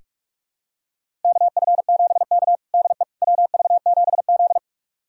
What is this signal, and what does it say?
Morse code, heard as a keyed tone, about 46 wpm, 690 Hz. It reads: KF7K DE WV7Z
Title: USA Callsign DE Callsign - 4 Characters - 45wpm